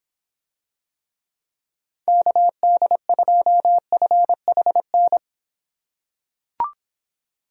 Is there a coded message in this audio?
Yes: Morse code from 2.1 to 5.2 s, heard as a keyed tone, about 26 wpm, 700 Hz. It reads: KD2FHN